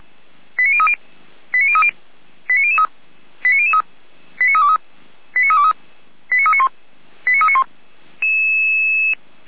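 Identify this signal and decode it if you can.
Five-tone selective call (ZVEI), heard as quick runs of five tones, tones of about 70 ms. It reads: 89030 89030 89003 89003 89323 89323 89391 89391
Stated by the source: ZVEI five-tone selective call 89512, received by SDR.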